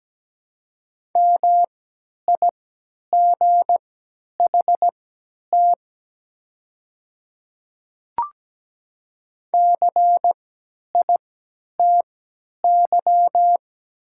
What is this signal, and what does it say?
Morse code, heard as a keyed tone, about 17 wpm, spaced as for 10 wpm, 700 Hz. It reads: MIGHT CITY
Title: 3x Character Spacing - Top 300 Words - 17wpm